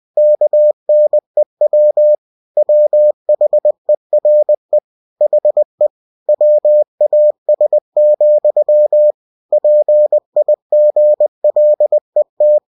Morse code, heard as a keyed tone, about 20 wpm, 600 Hz. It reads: KNEW WHERE HE WAS, PIGLET
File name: cw-books / winnie-the-pooh / winnie-20wpm-page-0072